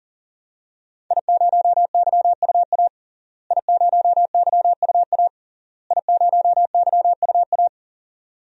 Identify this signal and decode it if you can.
Morse code, heard as a keyed tone, about 40 wpm, 700 Hz. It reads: I0YUA I0YUA I0YUA
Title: DXCC-100 Call Signs - 5 Characters - Repeated 3x - 40wpm